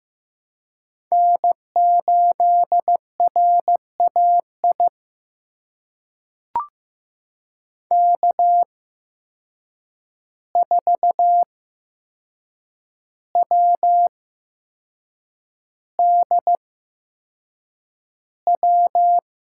Morse code, heard as a keyed tone, about 15 wpm, spaced as for 6 wpm, 700 Hz. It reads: N8RAI K4WDW
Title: USA Call Signs - 5 Characters - 8x Character Spacing - 15wpm